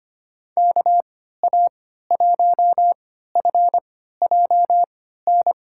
Morse code, heard as a keyed tone, about 25 wpm, spaced as for 14 wpm, 700 Hz. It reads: KA1FJN
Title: USA Call Signs - 6 Characters - 3x Character Spacing - 25wpm